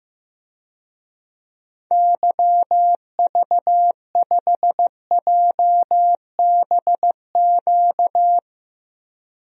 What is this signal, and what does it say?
Morse code, heard as a keyed tone, about 16 wpm, 700 Hz. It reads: YV5JBQ
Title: DXCC-100 Call Signs - 6 Characters - Repeated 2x - 15wpm